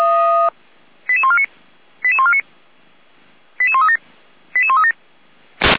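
Five-tone selective call (ZVEI), heard as quick runs of five tones, tones of about 70 ms. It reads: 88149 88149 88137 88137
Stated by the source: ZVEI five-tone selective call received on an SDR.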